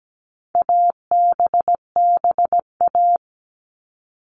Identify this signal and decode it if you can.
Morse code, heard as a keyed tone, about 17 wpm, 700 Hz. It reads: ABBA
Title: All Cut Numbers - 4-Digit 17wpm